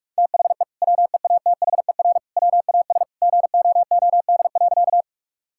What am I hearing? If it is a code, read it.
Morse code, heard as a keyed tone, about 42 wpm, 700 Hz. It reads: THE WEATHER WAS GOOD.
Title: Sentences from Top 1000 Words - Review Part 5 - 45wpm